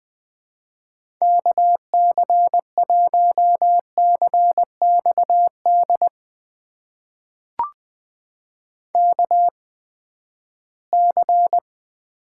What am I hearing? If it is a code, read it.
Morse code, heard as a keyed tone, about 20 wpm, spaced as for 14 wpm, 700 Hz. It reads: KC1CXD KC
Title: USA Call Signs - 6 Characters - 8x Character Spacing - 20wpm